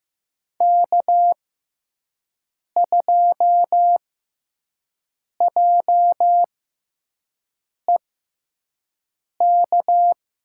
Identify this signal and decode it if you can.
Morse code, heard as a keyed tone, about 15 wpm, spaced as for 5 wpm, 700 Hz. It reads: K2JEK
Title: USA Call Signs - 6x Character Spacing - 15wpm